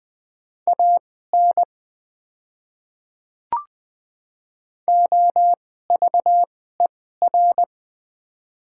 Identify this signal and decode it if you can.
Morse code, heard as a keyed tone, about 20 wpm, spaced as for 14 wpm, 700 Hz. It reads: AN OVER
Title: 2x Character Spacing - Top 100 Words - 20wpm